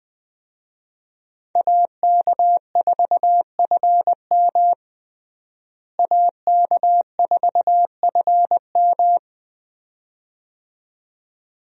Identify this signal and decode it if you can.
Morse code, heard as a keyed tone, about 20 wpm, 700 Hz. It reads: AK4FM AK4FM
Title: USA Call Signs - 5 Characters - Repeated 2x - 20wpm